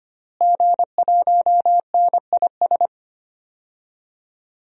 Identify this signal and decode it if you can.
Morse code, heard as a keyed tone, about 25 wpm, 700 Hz. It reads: G1NIS
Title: DXCC-100 Call Signs - 5 Characters - Repeated 3x - 25wpm